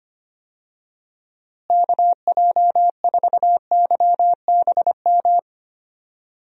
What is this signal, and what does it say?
Morse code, heard as a keyed tone, about 25 wpm, 700 Hz. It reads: KJ4YBM
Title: USA Call Signs - 6 Characters - Repeated 2x - 25wpm